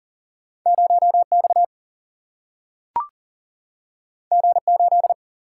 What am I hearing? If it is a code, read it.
Morse code, heard as a keyed tone, about 40 wpm, 700 Hz. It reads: 0 X G 8
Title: Single-Letter Single-Number 40wpm